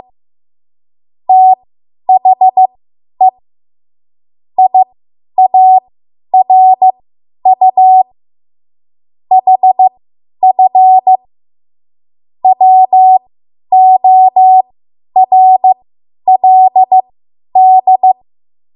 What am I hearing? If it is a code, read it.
Morse code, heard as a keyed tone, about 15 wpm, spaced as for 10 wpm, 750 Hz. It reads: THE IARU HF WORLD
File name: Morse / 10WPM